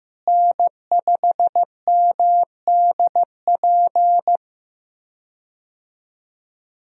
Morse code, heard as a keyed tone, about 15 wpm, 700 Hz. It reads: N5MDP
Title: USA Call Signs - 5 Characters - Repeated 3x - 15wpm